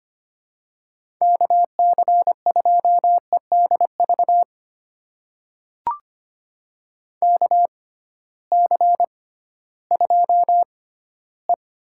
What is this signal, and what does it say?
Morse code, heard as a keyed tone, about 25 wpm, spaced as for 14 wpm, 700 Hz. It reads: KC2EDV KC2E
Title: USA Call Signs - 6 Characters - 6x Character Spacing - 25wpm